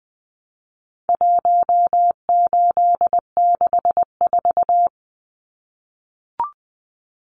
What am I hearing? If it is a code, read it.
Morse code, heard as a keyed tone, about 19 wpm, 700 Hz. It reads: 1864
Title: Numbers - 4-Digit 20wpm